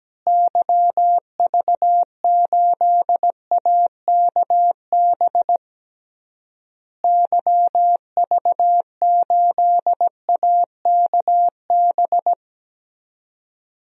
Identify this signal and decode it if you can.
Morse code, heard as a keyed tone, about 17 wpm, 700 Hz. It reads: YV8AKB YV8AKB
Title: DXCC-100 Call Signs - 6 Characters - Repeated 3x - 17wpm